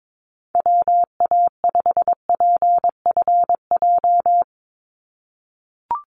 Morse code, heard as a keyed tone, about 22 wpm, 700 Hz. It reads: WA5PFJ
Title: Call Signs - 22wpm